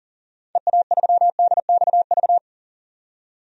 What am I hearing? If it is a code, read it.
Morse code, heard as a keyed tone, about 37 wpm, 700 Hz. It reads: EA3DXV
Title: DXCC-100 Call Signs - 6 Characters - 40wpm